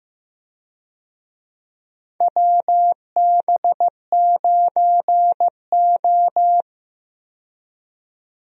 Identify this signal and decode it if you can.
Morse code, heard as a keyed tone, about 15 wpm, 700 Hz. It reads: WB9O
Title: USA Call Signs - 4 Characters - 15wpm